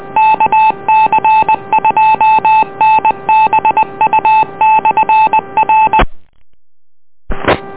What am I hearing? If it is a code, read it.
Morse code, heard as a keyed tone, about 20 wpm, 870 Hz. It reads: KC2NBU/R